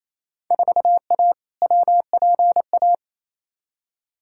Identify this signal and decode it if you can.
Morse code, heard as a keyed tone, about 27 wpm, 700 Hz. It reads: 4A WPA
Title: Field Day Exchanges Encore - 28wpm